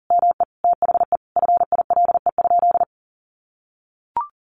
Morse code, heard as a keyed tone, about 40 wpm, 700 Hz. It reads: ME THE FILE?
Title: Sentences from Top 1200 Words 40wpm